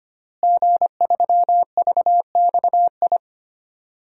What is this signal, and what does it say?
Morse code, heard as a keyed tone, about 25 wpm, 700 Hz. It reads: G3VXI
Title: DXCC-100 Call Signs - 5 Characters - Repeated 2x - 25wpm